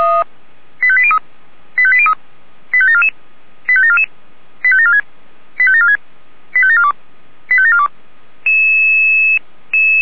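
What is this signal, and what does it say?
Five-tone selective call (ZVEI), heard as quick runs of five tones, tones of about 70 ms. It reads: 85803 85803 85744 85744 85746 85746 85742 85742
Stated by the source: ZVEI five-tone selective call received on an SDR.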